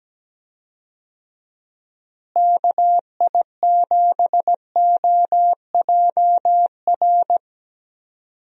Morse code, heard as a keyed tone, about 17 wpm, 700 Hz. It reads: KI7OJR
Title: USA Call Signs - 6 Characters - Repeated 2x - 17wpm